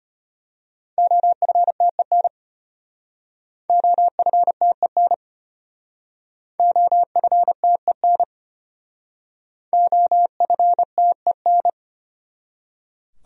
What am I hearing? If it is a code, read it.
Morse code, heard as a keyed tone, about 34 wpm, 700 Hz. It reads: OFTEN OFTEN OFTEN OFTEN